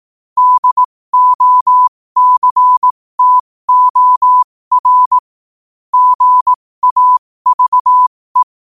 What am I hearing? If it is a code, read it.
Morse code, heard as a keyed tone, about 18 wpm, spaced as for 15 wpm, 1000 Hz. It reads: DOCTOR GAVE
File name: qotdfiles / 15wpm